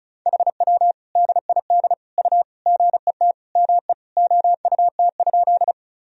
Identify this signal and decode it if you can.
Morse code, heard as a keyed tone, about 34 wpm, 690 Hz. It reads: HW DID U GET ME OUT?
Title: Sentences from Top 200 Words and Abbreviations 35wpm